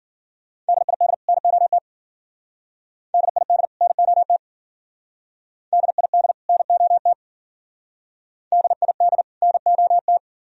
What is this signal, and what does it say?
Morse code, heard as a keyed tone, about 50 wpm, 700 Hz. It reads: DID NOT DID NOT DID NOT DID NOT
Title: Speed-Racing - Sets of 2 Words 40wpm